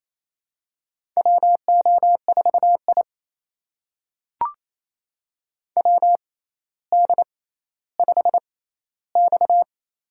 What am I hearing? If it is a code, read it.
Morse code, heard as a keyed tone, about 28 wpm, spaced as for 14 wpm, 700 Hz. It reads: WO4I WD5X
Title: USA Call Signs - 4 Characters - 6x Character Spacing - 28wpm